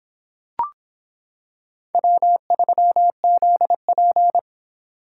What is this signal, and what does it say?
Morse code, heard as a keyed tone, about 25 wpm, 700 Hz. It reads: W3ZP